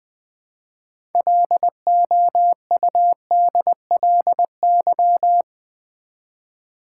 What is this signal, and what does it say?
Morse code, heard as a keyed tone, about 20 wpm, 690 Hz. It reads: LOUDLY
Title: Build A Word - 2 Letters At A Time - 6-Letter Words - 20wpm